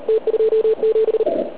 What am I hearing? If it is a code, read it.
Morse code, heard as a keyed tone, about 36 wpm, 440 Hz. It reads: T27